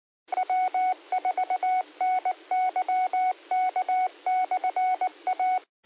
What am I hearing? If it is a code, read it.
Morse code, heard as a keyed tone, about 19 wpm, 740 Hz. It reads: W4NYK/A